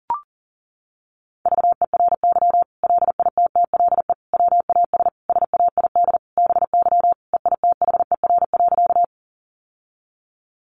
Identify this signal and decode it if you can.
Morse code, heard as a keyed tone, about 37 wpm, 700 Hz. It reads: VERY LITTLE WAS SAID BY EITHER.
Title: Sentences from Top 600 Words 40wpm